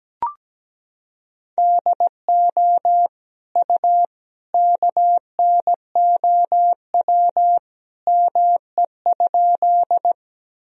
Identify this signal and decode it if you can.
Morse code, heard as a keyed tone, about 17 wpm, 700 Hz. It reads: DO U KNOW ME?
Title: Sentences from Top 200 Words and Abbreviations 17wpm